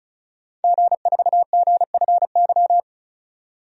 Morse code, heard as a keyed tone, about 36 wpm, 700 Hz. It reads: G4GFY
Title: DXCC-100 Call Signs - 5 Characters - 35wpm